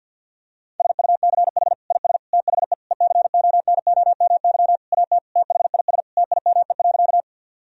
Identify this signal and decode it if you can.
Morse code, heard as a keyed tone, about 50 wpm, 700 Hz. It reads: SUCH IS THE ECONOMY AT THIS TIME.
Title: Sentences from Top 1000 Words - Review Part 3 - 50wpm